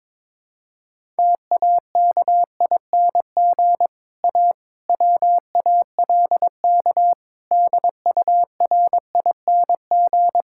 Morse code, heard as a keyed tone, about 22 wpm, 700 Hz. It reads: TAKING A WALK DURING
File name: Difficult Sentences with Hints - 22wpm